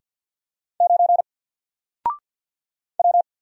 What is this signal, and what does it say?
Morse code, heard as a keyed tone, about 50 wpm, 700 Hz. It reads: 9 W
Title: Single Letter-Number Alt 50wpm